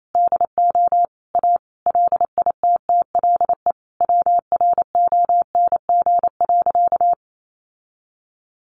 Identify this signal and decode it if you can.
Morse code, heard as a keyed tone, about 28 wpm, 690 Hz. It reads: DO A LITTLE WRONG.
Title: Sentences from Top 900 Words 28wpm